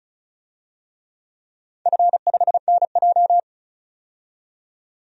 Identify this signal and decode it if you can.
Morse code, heard as a keyed tone, about 36 wpm, 690 Hz. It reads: F5NJ
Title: DXCC-100 Call Signs - 4 Characters - 35wpm